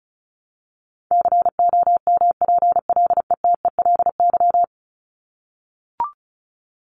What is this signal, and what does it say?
Morse code, heard as a keyed tone, about 34 wpm, 700 Hz. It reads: COMPLETELY
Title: LY Suffix - Top 100 Words with Base Words 35wpm